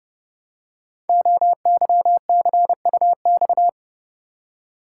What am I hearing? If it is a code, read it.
Morse code, heard as a keyed tone, about 30 wpm, 700 Hz. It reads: OYCUX
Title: Code Groups - 5 Letter:Numbers - 30wpm